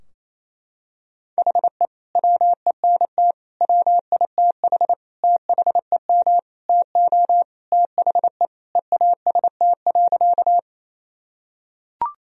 Morse code, heard as a keyed tone, about 28 wpm, 700 Hz. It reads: HE WENT WITH THEM TO THE EAST.
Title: Sentences from Top 1000 Words - Review Part 3 - 28wpm